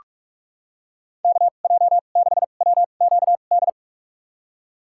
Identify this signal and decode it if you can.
Morse code, heard as a keyed tone, about 45 wpm, spaced as for 32 wpm, 700 Hz. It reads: KJ6WQD
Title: USA Call Signs - 2x Character Spacing - 45wpm